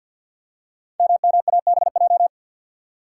Morse code, heard as a keyed tone, about 47 wpm, 700 Hz. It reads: MMABJ
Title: Code Groups - 5 Letter:Numbers - 50wpm